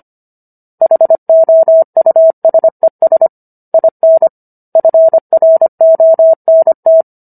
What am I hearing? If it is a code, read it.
Morse code, heard as a keyed tone, about 25 wpm, 650 Hz. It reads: HOUSES IN FRONT